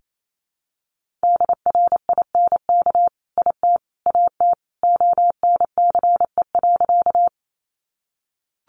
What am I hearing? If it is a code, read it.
Morse code, heard as a keyed tone, about 28 wpm, 700 Hz. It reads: DRINK IT AT ONCE.
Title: Sentences from Top 1500 Words 28wpm